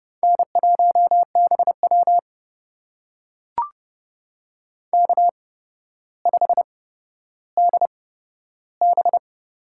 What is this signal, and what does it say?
Morse code, heard as a keyed tone, about 30 wpm, spaced as for 13 wpm, 700 Hz. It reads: N1BW K5DB
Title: USA Call Signs - 4 Characters - 8x Character Spacing - 30wpm